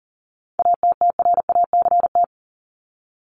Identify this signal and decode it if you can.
Morse code, heard as a keyed tone, about 40 wpm, 700 Hz. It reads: ATTRACT
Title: Top 1501-1750 Words 40wpm